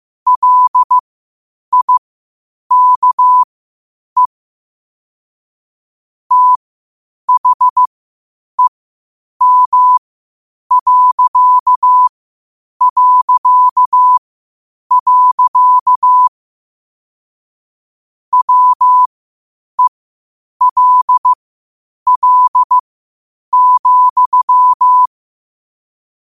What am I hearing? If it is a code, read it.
Morse code, heard as a keyed tone, about 15 wpm, spaced as for 8 wpm, 1000 Hz. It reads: LIKE THEM... WELL,
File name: qotdfiles / 8wpm